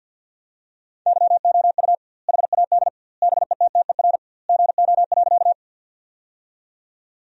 Morse code, heard as a keyed tone, about 51 wpm, 700 Hz. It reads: YOU HAD BETTER GO.
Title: 2x Word-Spacing - Sentences from Top 400 Words - 50wpm